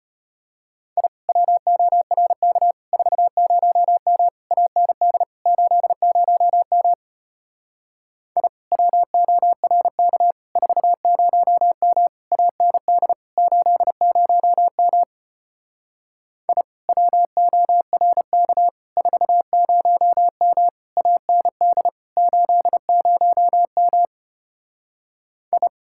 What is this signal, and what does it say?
Morse code, heard as a keyed tone, about 33 wpm, 700 Hz. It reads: I WORK 40M AND 80M I WORK 40M AND 80M I WORK 40M AND 80M I